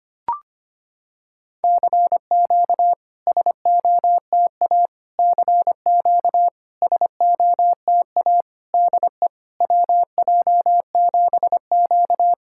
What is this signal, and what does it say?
Morse code, heard as a keyed tone, about 25 wpm, 690 Hz. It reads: CQ SOTA CQ SOTA DE WJ7Q
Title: SOTA QSO Practice 25wpm